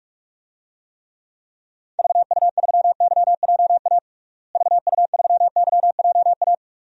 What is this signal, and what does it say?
Morse code, heard as a keyed tone, about 44 wpm, 700 Hz. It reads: VU3YJA VU3YJA
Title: DXCC-100 Call Signs - 6 Characters - Repeated 2x - 45wpm